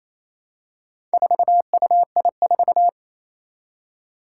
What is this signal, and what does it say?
Morse code, heard as a keyed tone, about 28 wpm, 700 Hz. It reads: 4UI4
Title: Four Challenging Letter-Number Combinations 28wpm